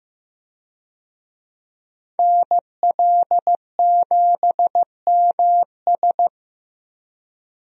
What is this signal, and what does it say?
Morse code, heard as a keyed tone, about 15 wpm, 700 Hz. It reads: NL7MS
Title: USA Call Signs - 5 Characters - Repeated 3x - 15wpm